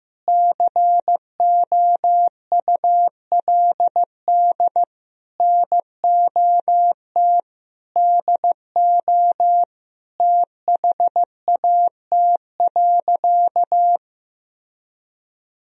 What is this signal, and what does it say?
Morse code, heard as a keyed tone, about 15 wpm, 700 Hz. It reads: COULD NOT DO THAT.
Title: Sentences from Top 1000 Words - Review Part 4 - 15wpm